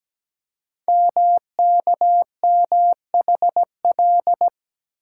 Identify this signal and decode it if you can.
Morse code, heard as a keyed tone, about 17 wpm, 700 Hz. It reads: MKMHL
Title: Code Groups - 5 Letters - 17wpm